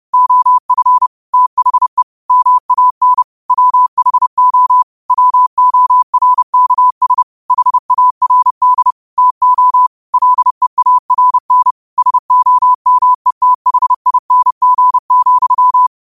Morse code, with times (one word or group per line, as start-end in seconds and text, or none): 0.1-1.1 OF
1.3-2.0 THE
2.3-3.2 MAN
3.5-4.8 WHO
5.1-7.2 WORKS
7.5-8.9 HARD
9.2-9.9 TO
10.1-11.7 LEARN
12.0-15.9 SOMETHING,